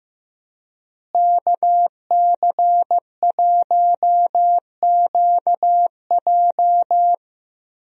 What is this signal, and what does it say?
Morse code, heard as a keyed tone, about 15 wpm, 700 Hz. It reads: KC1QJ
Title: USA Call Signs - 5 Characters - Repeated 2x - 15wpm